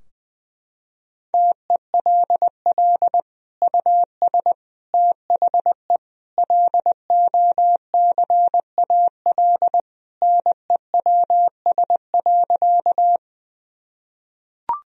Morse code, heard as a keyed tone, about 21 wpm, 700 Hz. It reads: TELL US THE LOCAL NEWS.
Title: Sentences from Top 1000 Words - Review Part 3 - 20wpm